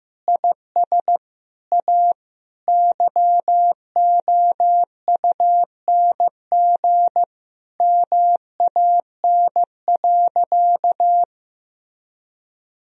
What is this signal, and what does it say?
Morse code, heard as a keyed tone, about 15 wpm, 700 Hz. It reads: IS A YOUNG MAN.
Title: Sentences from Top 1000 Words - Review Part 4 - 15wpm